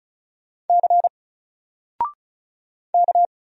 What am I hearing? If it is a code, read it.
Morse code, heard as a keyed tone, about 34 wpm, 700 Hz. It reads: C K
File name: Single-Letter-Number-35wpm